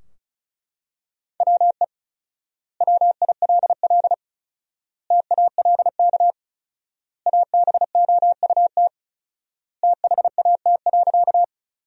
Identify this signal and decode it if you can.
Morse code, heard as a keyed tone, about 36 wpm, 700 Hz. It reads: WE WILL TALK ABOUT THAT.
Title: 4x Word-Spacing - Sentences from Top 400 Words - 35wpm